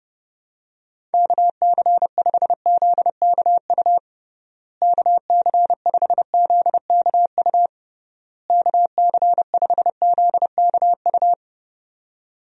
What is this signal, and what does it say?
Morse code, heard as a keyed tone, about 31 wpm, 700 Hz. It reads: KC5ZKU KC5ZKU KC5ZKU
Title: USA Call Signs - 6 Characters - Repeated 3x - 30wpm